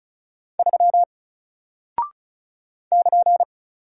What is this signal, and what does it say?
Morse code, heard as a keyed tone, about 36 wpm, 700 Hz. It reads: 3 <KN>
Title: Single Character (Letter:Number:Punctuation:Prosign) Alt 35wpm